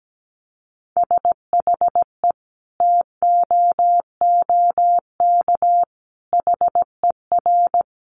Morse code, heard as a keyed tone, about 17 wpm, 700 Hz. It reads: SHE TOOK HER
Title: Sentences from Top 900 Words 17wpm